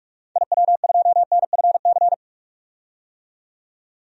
Morse code, heard as a keyed tone, about 44 wpm, 690 Hz. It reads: IW2NFC
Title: DXCC-100 Call Signs - 6 Characters - 45wpm